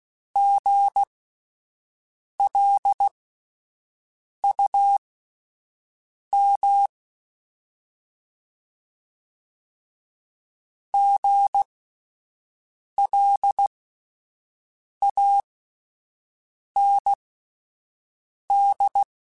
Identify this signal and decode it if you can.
Morse code, heard as a keyed tone, about 16 wpm, spaced as for 5 wpm, 800 Hz. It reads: GLUM GLAND